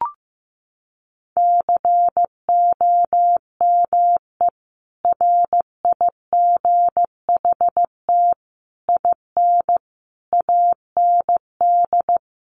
Morse code, heard as a keyed tone, about 15 wpm, 700 Hz. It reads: COME RIGHT IN AND
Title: Sentences from Top 600 Words 15wpm